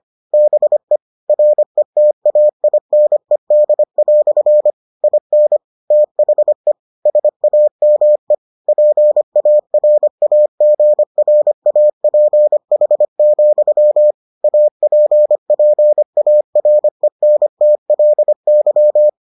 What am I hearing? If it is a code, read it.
Morse code, heard as a keyed tone, about 25 wpm, 600 Hz. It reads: BE RETAINED" IN THE SAME PARAGRAPH, APPARENTLY